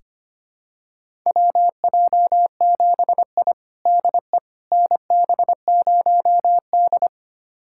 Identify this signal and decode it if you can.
Morse code, heard as a keyed tone, about 25 wpm, 700 Hz. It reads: WJ7I DE NB0D
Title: USA Callsign DE Callsign - 4 Characters - 25wpm